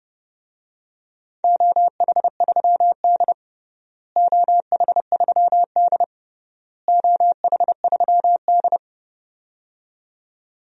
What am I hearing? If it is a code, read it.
Morse code, heard as a keyed tone, about 30 wpm, 700 Hz. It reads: OH3D OH3D OH3D
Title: DXCC-100 Call Signs - 4 Characters - Repeated 3x - 30wpm